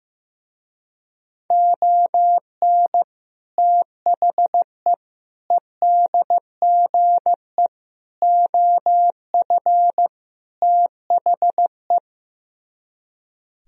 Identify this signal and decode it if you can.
Morse code, heard as a keyed tone, about 15 wpm, 700 Hz. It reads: ON THE EDGE OF THE
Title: Speed-Racing - Sets of 5 Words 15wpm